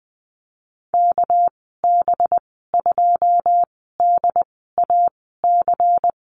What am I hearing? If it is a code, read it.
Morse code, heard as a keyed tone, about 20 wpm, spaced as for 14 wpm, 700 Hz. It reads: KB2DAC
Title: 207 Lesson 33g - Letter j - Call Signs 20wpm